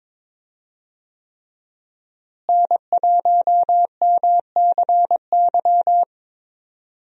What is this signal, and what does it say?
Morse code, heard as a keyed tone, about 22 wpm, 700 Hz. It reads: N1MCY
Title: USA Call Signs - 5 Characters - Repeated 3x - 22wpm